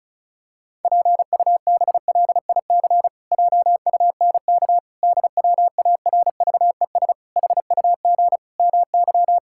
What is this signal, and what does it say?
Morse code, heard as a keyed tone, about 34 wpm, 700 Hz. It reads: PUBLIC JUNK DWARVES HUG MY
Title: Pangram Challenge v2 - 35wpm